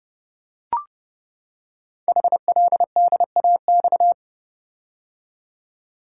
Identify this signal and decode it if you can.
Morse code, heard as a keyed tone, about 30 wpm, 700 Hz. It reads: HLDAX
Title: Code Groups - 5 Letters - 30wpm